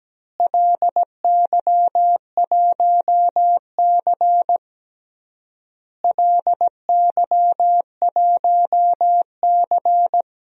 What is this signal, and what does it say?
Morse code, heard as a keyed tone, about 17 wpm, 700 Hz. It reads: LY1C LY1C
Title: DXCC-100 Call Signs - 4 Characters - Repeated 3x - 17wpm